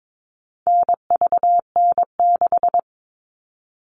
Morse code, heard as a keyed tone, about 22 wpm, 700 Hz. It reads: NVN6
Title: All Cut Numbers - 4-Digit 22wpm